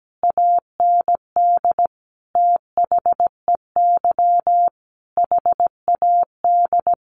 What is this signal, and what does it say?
Morse code, heard as a keyed tone, about 17 wpm, 700 Hz. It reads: AND THEY HAD
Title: Sentences from Top 1500 Words 17wpm